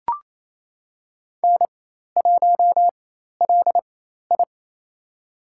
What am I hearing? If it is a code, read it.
Morse code, heard as a keyed tone, about 28 wpm, spaced as for 13 wpm, 700 Hz. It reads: N1LI
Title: USA Call Signs - 4x Character Spacing - 28wpm